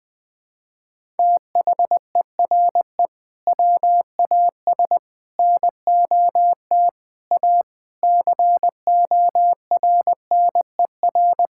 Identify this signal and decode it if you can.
Morse code, heard as a keyed tone, about 19 wpm, 700 Hz. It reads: THERE WAS NOT A CORNER
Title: Sentences from Top 1600 Words 20wpm